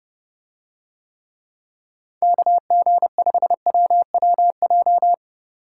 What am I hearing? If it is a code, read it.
Morse code, heard as a keyed tone, about 30 wpm, 700 Hz. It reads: KG5WWJ